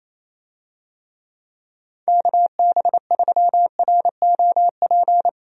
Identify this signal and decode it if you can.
Morse code, heard as a keyed tone, about 28 wpm, 700 Hz. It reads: KB3ROP
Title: USA Call Signs - 6 Characters - Repeated 3x - 28wpm